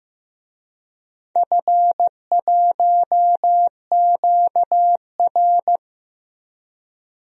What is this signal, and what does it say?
Morse code, heard as a keyed tone, about 15 wpm, 700 Hz. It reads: F1QR
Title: DXCC-100 Call Signs - 4 Characters - 15wpm